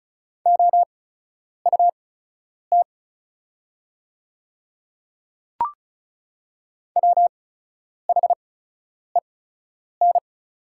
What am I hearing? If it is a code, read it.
Morse code, heard as a keyed tone, about 34 wpm, spaced as for 10 wpm, 690 Hz. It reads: OUT WHEN
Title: 8x Character Spacing - Top 100 Words - 35wpm